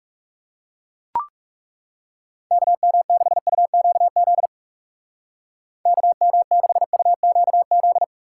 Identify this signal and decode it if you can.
Morse code, heard as a keyed tone, about 35 wpm, 700 Hz. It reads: KM6UQZ KM6UQZ
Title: Speed Racing - USA 2x3 Call Signs 30wpm